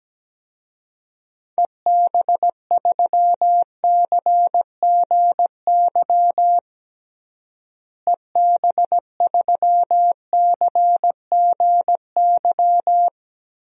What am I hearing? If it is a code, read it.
Morse code, heard as a keyed tone, about 17 wpm, 700 Hz. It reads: EB3CGY EB3CGY
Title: DXCC-100 Call Signs - 6 Characters - Repeated 3x - 17wpm